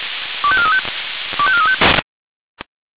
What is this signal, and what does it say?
Five-tone selective call (ZVEI), heard as quick runs of five tones, tones of about 70 ms. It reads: 25435 25435